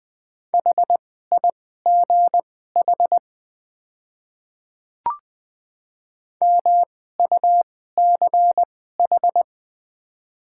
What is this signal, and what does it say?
Morse code, heard as a keyed tone, about 20 wpm, spaced as for 14 wpm, 700 Hz. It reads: HIGH MUCH
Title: 2x Character Spacing - Top 200 Words - 20wpm